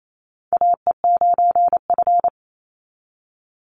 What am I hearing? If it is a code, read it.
Morse code, heard as a keyed tone, about 28 wpm, 700 Hz. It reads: AE9F
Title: NA SOTA Chaser Callsigns 28wpm